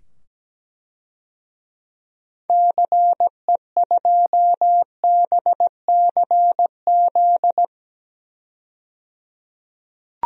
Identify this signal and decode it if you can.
Morse code, heard as a keyed tone, about 17 wpm, 700 Hz. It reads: CE2BCZ
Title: DXCC-100 Call Signs - 6 Characters - 17wpm